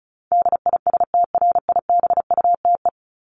Morse code, heard as a keyed tone, about 34 wpm, 700 Hz. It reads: DISTRIBUTE
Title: Top 1501-1750 Words 35wpm